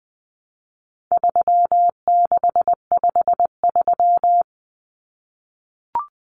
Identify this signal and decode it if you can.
Morse code, heard as a keyed tone, about 20 wpm, 700 Hz. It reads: 3653
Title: Common Cut Numbers - 4-Digit 20wpm